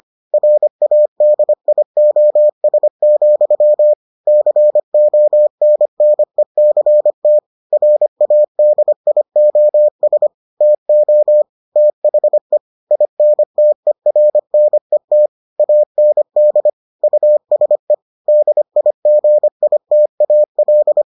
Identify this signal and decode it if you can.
Morse code, heard as a keyed tone, about 25 wpm, 600 Hz. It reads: RADIOS, CONNECT RADIOS TO THE INTERNET AND USE DIGITAL